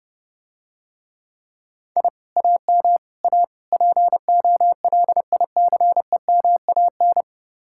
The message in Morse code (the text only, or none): I AM A POLICEMAN